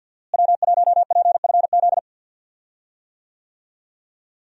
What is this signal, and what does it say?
Morse code, heard as a keyed tone, about 49 wpm, 700 Hz. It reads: W1PFZ
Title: USA Call Signs - 5 Characters - Repeated 3x - 50wpm